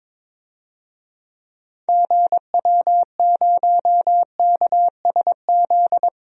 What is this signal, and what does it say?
Morse code, heard as a keyed tone, about 22 wpm, 700 Hz. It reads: GW0KSZ
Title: DXCC-100 Call Signs - 6 Characters - Repeated 3x - 22wpm